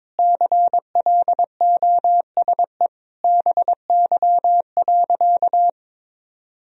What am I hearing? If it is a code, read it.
Morse code, heard as a keyed tone, about 22 wpm, 700 Hz. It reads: CLOSE BY.
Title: Sentences from Top 1000 Words - Review Part 4 - 22wpm